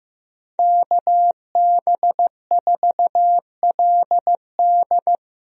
Morse code, heard as a keyed tone, about 14 wpm, 700 Hz. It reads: KB4LD
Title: USA Call Signs - 5 Characters - Repeated 2x - 15wpm